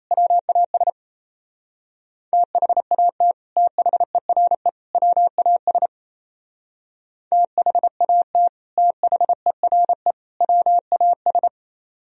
Morse code, heard as a keyed tone, about 34 wpm, 700 Hz. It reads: WAS THAT THERE WAS THAT THERE WAS
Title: Speed-Racing - Sets of 3 Words 28wpm